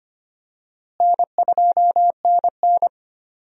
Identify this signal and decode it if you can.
Morse code, heard as a keyed tone, about 25 wpm, 700 Hz. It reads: N2NN
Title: USA Call Signs - 4 Characters - Repeated 3x - 25wpm